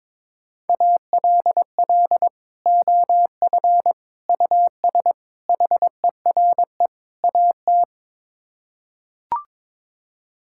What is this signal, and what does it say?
Morse code, heard as a keyed tone, about 22 wpm, 700 Hz. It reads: ALL OF US HERE AT
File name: practice / 22wpm / Build A Phrase - 5-Word Phrases - 22wpm